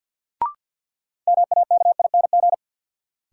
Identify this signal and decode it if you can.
Morse code, heard as a keyed tone, about 53 wpm, 700 Hz. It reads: MAKING
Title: Sentences from Top 1000 Words - Review Part 6 - New Word Drill - 50wpm